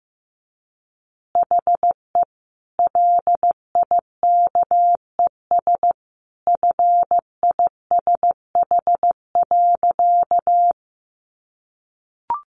Morse code, heard as a keyed tone, about 15 wpm, 700 Hz. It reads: HE LIKES FISH.
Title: Easy Phrases 15wpm